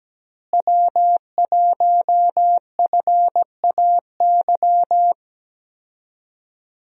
Morse code, heard as a keyed tone, about 17 wpm, 700 Hz. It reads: W1FAY
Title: USA Call Signs - 5 Characters - Repeated 2x - 17wpm